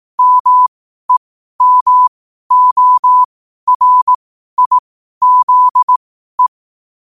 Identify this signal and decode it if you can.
Morse code, heard as a keyed tone, about 18 wpm, spaced as for 12 wpm, 1000 Hz. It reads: MEMORIZE
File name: qotdfiles / 12wpm